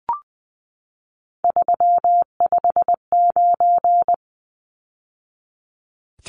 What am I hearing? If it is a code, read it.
Morse code, heard as a keyed tone, about 20 wpm, 700 Hz. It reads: 359
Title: Numbers - 3-Digit 20wpm